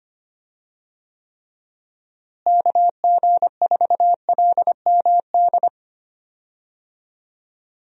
Morse code, heard as a keyed tone, about 25 wpm, 700 Hz. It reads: KG4LMD
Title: USA Call Signs - 6 Characters - Repeated 2x - 25wpm